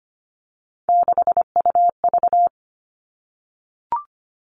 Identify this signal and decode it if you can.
Morse code, heard as a keyed tone, about 25 wpm, 700 Hz. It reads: 6UV
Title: All Cut Numbers - 3-Digit 25wpm